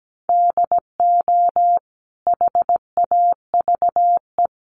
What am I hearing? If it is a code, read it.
Morse code, heard as a keyed tone, about 17 wpm, 700 Hz. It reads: DO HAVE
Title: Sentences from Top 1300 Words 17wpm